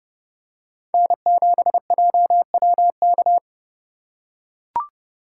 Morse code, heard as a keyed tone, about 31 wpm, 700 Hz. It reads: N7JWK